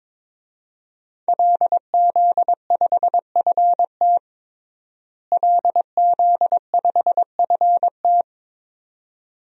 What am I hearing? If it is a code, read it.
Morse code, heard as a keyed tone, about 22 wpm, 700 Hz. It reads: LZ5FT LZ5FT
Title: DXCC-100 Call Signs - 5 Characters - Repeated 2x - 22wpm